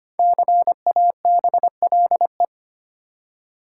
Morse code, heard as a keyed tone, about 25 wpm, 700 Hz. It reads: CABLE
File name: 5 Letter Words - First 3 Letters Given - 25wpm